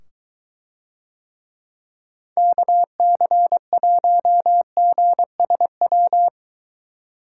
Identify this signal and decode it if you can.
Morse code, heard as a keyed tone, about 23 wpm, 700 Hz. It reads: KC1GSW